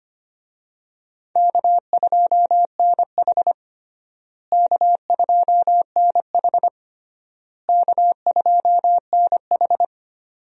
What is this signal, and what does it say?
Morse code, heard as a keyed tone, about 25 wpm, 700 Hz. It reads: K2NH K2NH K2NH
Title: USA Call Signs - 4 Characters - Repeated 3x - 25wpm